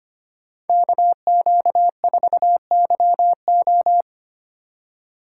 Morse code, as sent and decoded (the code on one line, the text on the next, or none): -.- --.- ....- -.-- ---
KQ4YO